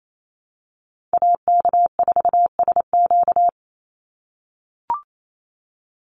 Morse code, heard as a keyed tone, about 28 wpm, 700 Hz. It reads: AK4SQ
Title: Call Signs - 28wpm